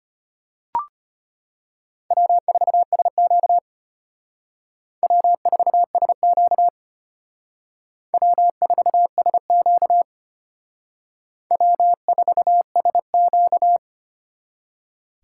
Morse code, heard as a keyed tone, about 33 wpm, 700 Hz. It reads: W4SQ W4SQ W4SQ W4SQ